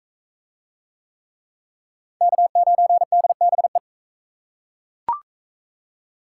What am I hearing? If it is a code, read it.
Morse code, heard as a keyed tone, about 36 wpm, 700 Hz. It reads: K9DBE